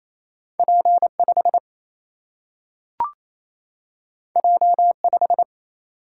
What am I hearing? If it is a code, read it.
Morse code, heard as a keyed tone, about 28 wpm, 700 Hz. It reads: P5 J5
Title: Single-Letter Single-Number 28wpm